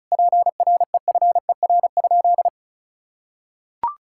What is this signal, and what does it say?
Morse code, heard as a keyed tone, about 36 wpm, 700 Hz. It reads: PREFER?
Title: Sentences from Top 1600 Words 35wpm